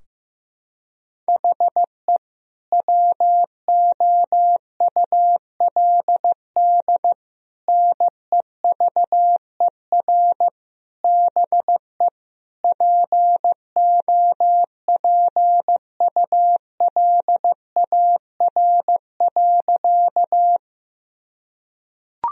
Morse code, heard as a keyed tone, about 15 wpm, 700 Hz. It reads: HE WOULD NEVER BE POPULAR.
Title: Sentences from Top 1000 Words - Review Part 3 - 15wpm